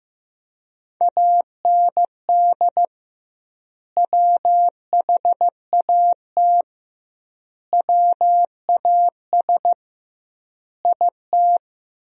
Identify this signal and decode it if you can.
Morse code, heard as a keyed tone, about 16 wpm, 700 Hz. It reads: AND WHAT WAS IT
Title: 2x Word-Spacing - Sentences from Top 100 Words - 15wpm